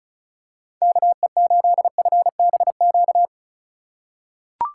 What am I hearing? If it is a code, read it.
Morse code, heard as a keyed tone, about 36 wpm, 700 Hz. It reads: KE8FBQ
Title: USA Call Signs - 6 Characters - 3x Character Spacing - 35wpm